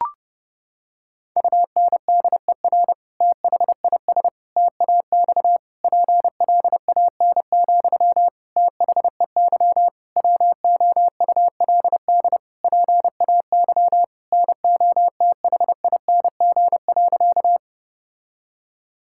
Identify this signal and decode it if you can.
Morse code, heard as a keyed tone, about 30 wpm, 700 Hz. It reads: UNDER THIS TAX PLAN, THEY WOULD PAY NOTHING.
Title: Sentences from Top 1000 Words - Review Part 5 - 30wpm